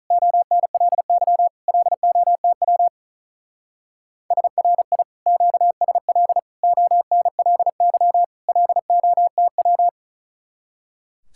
Morse code, heard as a keyed tone, about 44 wpm, 700 Hz. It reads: ONLY LOTW SRI QSL ONLY LOTW